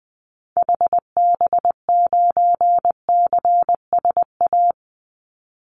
Morse code, heard as a keyed tone, about 20 wpm, 700 Hz. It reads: HB9CSA
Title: Call Signs - 20wpm